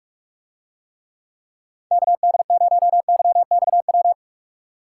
Morse code, heard as a keyed tone, about 47 wpm, 700 Hz. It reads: KD0YXW